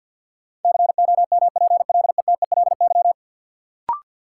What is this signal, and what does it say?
Morse code, heard as a keyed tone, about 51 wpm, 700 Hz. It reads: COMPLETELY
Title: Sentences from Top 1100 Words - New Word Drill Alt - 50wpm